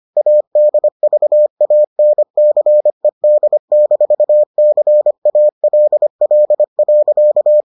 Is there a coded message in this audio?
Yes: Morse code from 0.2 to 7.6 s, heard as a keyed tone, about 25 wpm, 600 Hz. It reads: ADVANCED-CALL.